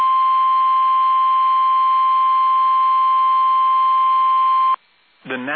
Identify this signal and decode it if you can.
Emergency Alert System (SAME) tones (the 1050 Hz attention tone), with speech. It is heard from 0.4 to 4.4 s.